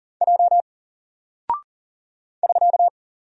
Morse code, heard as a keyed tone, about 39 wpm, 700 Hz. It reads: J <SK>
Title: Single Character (Letter:Number:Punctuation:Prosign) 40wpm